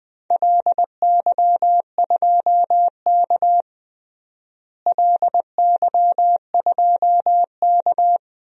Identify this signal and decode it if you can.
Morse code, heard as a keyed tone, about 20 wpm, 700 Hz. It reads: LY2K LY2K
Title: DXCC-100 Call Signs - 4 Characters - Repeated 3x - 20wpm